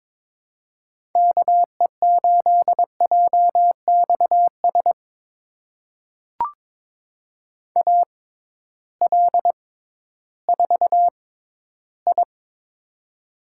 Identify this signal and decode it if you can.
Morse code, heard as a keyed tone, about 22 wpm, spaced as for 13 wpm, 700 Hz. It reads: KE8JXS AL4I
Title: USA Call Signs - 6x Character Spacing - 22wpm